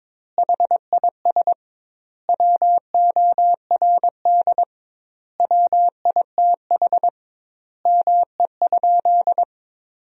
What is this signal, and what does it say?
Morse code, heard as a keyed tone, about 22 wpm, 700 Hz. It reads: HIS WORD WITH ME?
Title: 2x Word-Spacing - Sentences from Top 300 Words - 22wpm